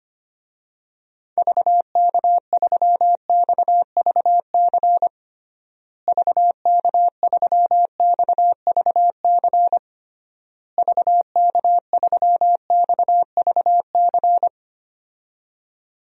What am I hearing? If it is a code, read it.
Morse code, heard as a keyed tone, about 25 wpm, 700 Hz. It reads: VK3XVC VK3XVC VK3XVC
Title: DXCC-100 Call Signs - 6 Characters - Repeated 3x - 25wpm